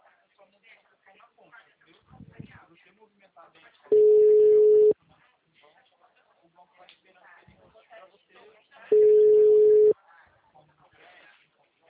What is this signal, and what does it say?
Ringback tone (Europe, 425 Hz), with speech.